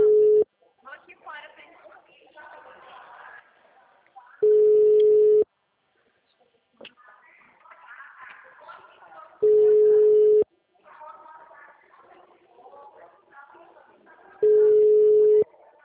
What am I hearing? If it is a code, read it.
Ringback tone (Europe, 425 Hz), with speech.